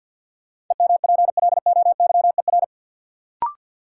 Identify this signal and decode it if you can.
Morse code, heard as a keyed tone, about 48 wpm, 700 Hz. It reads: EMPLOYER
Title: Sentences from Top 1600 Words - New Word Drill - 50wpm